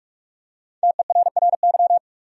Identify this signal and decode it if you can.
Morse code, heard as a keyed tone, about 45 wpm, 700 Hz. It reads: TERRY
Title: USA Common Names - Max 5 Letters - 45wpm